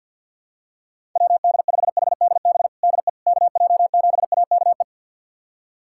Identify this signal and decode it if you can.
Morse code, heard as a keyed tone, about 46 wpm, 700 Hz. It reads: WD5HDB DE KJ7AKE
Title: USA Callsign DE Callsign - 6 Characters - 50wpm